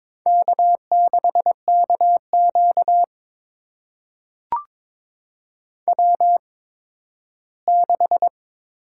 Morse code, heard as a keyed tone, about 22 wpm, spaced as for 13 wpm, 700 Hz. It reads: K6KQ W6
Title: USA Call Signs - 4 Characters - 8x Character Spacing - 22wpm